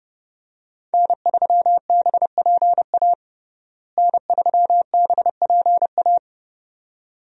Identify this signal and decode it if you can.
Morse code, heard as a keyed tone, about 30 wpm, 700 Hz. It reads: N3BPA N3BPA
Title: USA Call Signs - 5 Characters - Repeated 2x - 30wpm